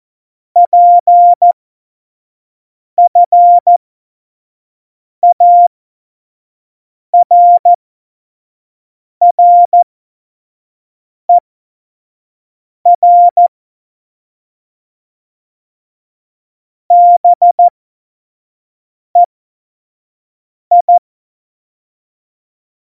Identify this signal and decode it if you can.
Morse code, heard as a keyed tone, about 14 wpm, spaced as for 5 wpm, 700 Hz. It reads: PFARRER BEI
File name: Morsekurs / Witze / Witz 06. Oktober 2020, 25 F70